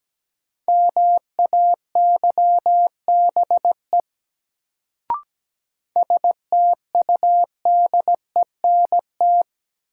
Morse code, heard as a keyed tone, about 17 wpm, 700 Hz. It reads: MAYBE STUDENT
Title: Sentences from Top 1000 Words - New Word Drill - 17wpm